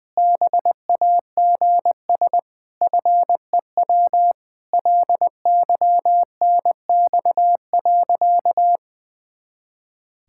Morse code, heard as a keyed tone, about 20 wpm, 700 Hz. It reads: BAGS FEW LYNX.
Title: Pangram Challenge v2 - 20wpm